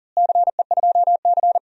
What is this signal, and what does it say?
Morse code, heard as a keyed tone, about 40 wpm, 700 Hz. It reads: CE2C
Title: DXCC-100 Call Signs - 4 Characters - Repeated 2x - 40wpm